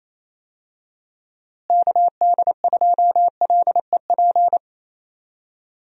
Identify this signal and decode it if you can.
Morse code, heard as a keyed tone, about 28 wpm, 700 Hz. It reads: KD2LEP
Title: USA Call Signs - 6 Characters - Repeated 3x - 28wpm